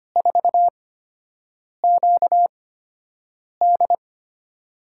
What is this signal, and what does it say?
Morse code, heard as a keyed tone, about 25 wpm, spaced as for 7 wpm, 700 Hz. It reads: 4QD
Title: USA Call Signs - 5 Characters - 8x Character Spacing - 25wpm